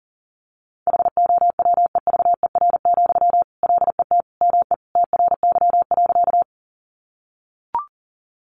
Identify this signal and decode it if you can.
Morse code, heard as a keyed tone, about 40 wpm, 700 Hz. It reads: HOWEVER, LET ME TRY.
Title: Sentences from Top 600 Words 40wpm